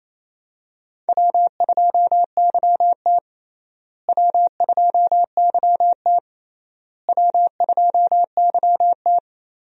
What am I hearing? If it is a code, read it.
Morse code, heard as a keyed tone, about 28 wpm, 700 Hz. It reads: W2YT W2YT W2YT
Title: USA Call Signs - 4 Characters - Repeated 3x - 28wpm